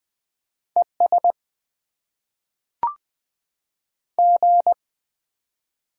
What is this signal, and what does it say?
Morse code, heard as a keyed tone, about 20 wpm, 700 Hz. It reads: ES G
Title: Top 20 QSO Elements - 8x Character Spacing - 20wpm